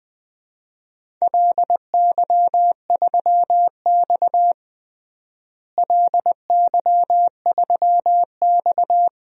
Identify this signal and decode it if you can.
Morse code, heard as a keyed tone, about 20 wpm, 700 Hz. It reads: LY3X LY3X
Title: DXCC-100 Call Signs - 4 Characters - Repeated 3x - 20wpm